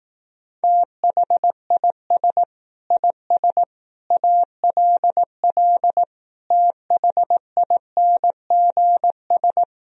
Morse code, heard as a keyed tone, about 18 wpm, 700 Hz. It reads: THIS IS ALL THINGS
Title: Speed-Racing - Sets of 5 Words 15wpm